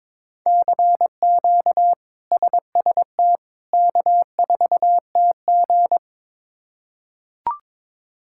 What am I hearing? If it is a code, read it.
Morse code, heard as a keyed tone, about 22 wpm, 700 Hz. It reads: CQ SST K4TG
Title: K1USN SST QSO Practice - 22wpm